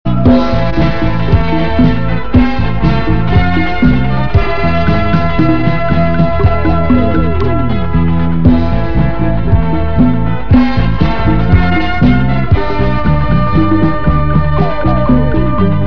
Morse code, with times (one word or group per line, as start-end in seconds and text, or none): none